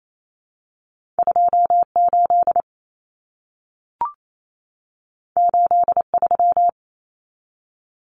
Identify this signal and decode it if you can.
Morse code, heard as a keyed tone, about 28 wpm, 700 Hz. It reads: 28 83
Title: Numbers - 2-Digit 28wpm